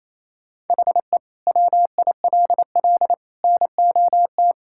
Morse code, heard as a keyed tone, about 28 wpm, 700 Hz. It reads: HE WILL NOT
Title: Sentences from Top 1000 Words - Review Part 6 - 28wpm